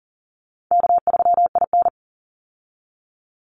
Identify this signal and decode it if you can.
Morse code, heard as a keyed tone, about 40 wpm, 700 Hz. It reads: K3IN
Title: Easy Call Signs Repeated 3x - 40wpm